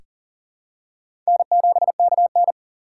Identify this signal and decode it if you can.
Morse code, heard as a keyed tone, about 40 wpm, 700 Hz. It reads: N7KN
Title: USA Call Signs - 3x Character Spacing - 40wpm